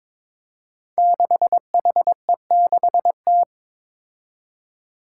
Morse code, heard as a keyed tone, about 22 wpm, 700 Hz. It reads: 6HE6T
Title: Five Challenging Letter-Number Combinations 22wpm